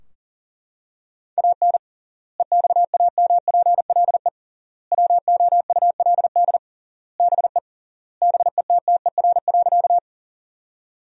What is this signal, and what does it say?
Morse code, heard as a keyed tone, about 40 wpm, 700 Hz. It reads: AN EXAMPLE WOULD BE BETTER.
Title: 3x Word-Spacing - Sentences from Top 400 Words - 40wpm